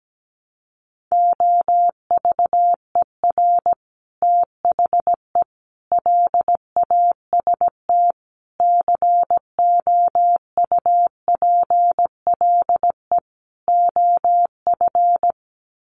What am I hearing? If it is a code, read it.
Morse code, heard as a keyed tone, about 17 wpm, 700 Hz. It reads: OVER THE LAST COUPLE OF
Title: Sets of 5 Words Encore 17wpm